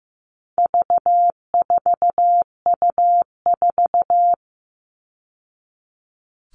Morse code, heard as a keyed tone, about 15 wpm, 700 Hz. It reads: V4U4
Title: All Cut Numbers - 4-Digit 15wpm